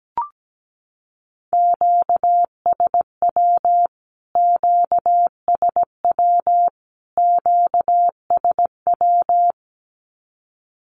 Morse code, heard as a keyed tone, about 17 wpm, 700 Hz. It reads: QSW QSW QSW
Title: Q-Codes Repeated 3x 17wpm